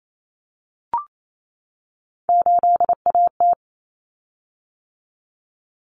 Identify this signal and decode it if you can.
Morse code, heard as a keyed tone, about 28 wpm, 700 Hz. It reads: 8AT
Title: Common Cut Numbers - 3-Digit 28wpm